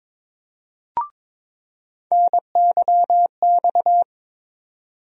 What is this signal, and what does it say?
Morse code, heard as a keyed tone, about 22 wpm, 700 Hz. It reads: NYX
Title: USA Call Sign Postfixes - 3 Letters 22wpm